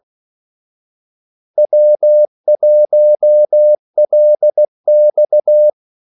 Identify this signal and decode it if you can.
Morse code, heard as a keyed tone, about 16 wpm, 600 Hz. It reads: W1LX
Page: Kaldesignaler 16 wpm